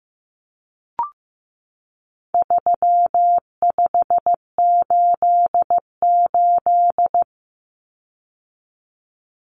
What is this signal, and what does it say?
Morse code, heard as a keyed tone, about 15 wpm, 700 Hz. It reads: 3588
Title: Common Cut Numbers - 4-Digit 15wpm